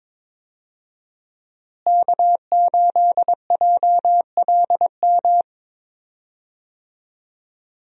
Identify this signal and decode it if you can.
Morse code, heard as a keyed tone, about 22 wpm, 700 Hz. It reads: K8JLM
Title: USA Call Signs - 5 Characters - Repeated 2x - 22wpm